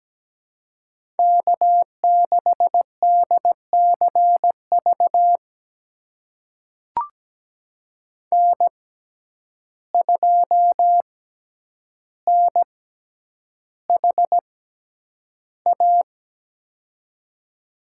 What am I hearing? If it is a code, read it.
Morse code, heard as a keyed tone, about 17 wpm, spaced as for 9 wpm, 700 Hz. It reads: K6DCV N2NHA
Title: USA Call Signs - 5 Characters - 6x Character Spacing - 17wpm